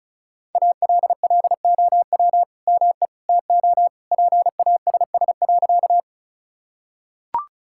Morse code, heard as a keyed tone, about 34 wpm, 690 Hz. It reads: ALLOW ME TO PASS.
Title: Sentences from Top 1000 Words - Review Part 5 - 35wpm